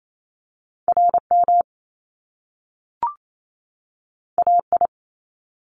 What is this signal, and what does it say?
Morse code, heard as a keyed tone, about 28 wpm, 700 Hz. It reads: RM AI
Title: Top Two Letter Combinations in Words 28wpm